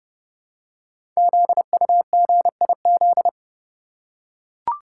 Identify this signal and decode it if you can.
Morse code, heard as a keyed tone, about 30 wpm, 700 Hz. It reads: ZUGIZ
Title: Code Groups - 5 Letter:Numbers - 30wpm